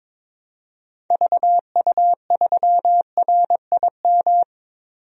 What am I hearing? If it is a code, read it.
Morse code, heard as a keyed tone, about 21 wpm, 700 Hz. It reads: VU3RIM
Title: DXCC-100 Call Signs - 6 Characters - 22wpm